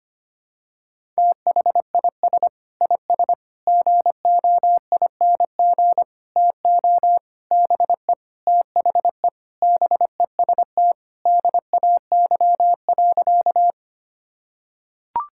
Morse code, heard as a keyed tone, about 25 wpm, 700 Hz. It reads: THIS IS GOING TO BE THE BEST DAY.
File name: practice / 25wpm / Sentences with 4-Word Phrases - 25wpm